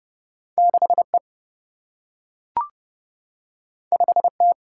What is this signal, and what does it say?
Morse code, heard as a keyed tone, about 32 wpm, 700 Hz. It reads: 6 E 5 T
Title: Single-Letter Single-Number 30wpm